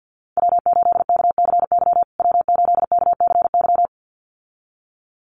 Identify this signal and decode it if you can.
Morse code, heard as a keyed tone, about 50 wpm, 700 Hz. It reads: W8KCY W8KCY
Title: USA Call Signs Repeated 2x 50wpm